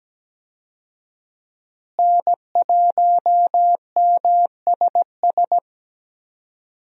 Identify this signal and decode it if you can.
Morse code, heard as a keyed tone, about 17 wpm, 700 Hz. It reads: N1MSS
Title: USA Call Signs - 5 Characters - Repeated 2x - 17wpm